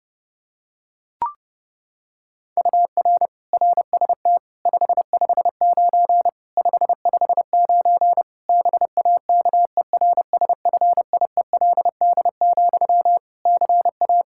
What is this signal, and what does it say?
Morse code, heard as a keyed tone, about 28 wpm, 700 Hz. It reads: UR RST 559 559 BAKERSFIELD, CA
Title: K1USN SST QSO Practice 30wpm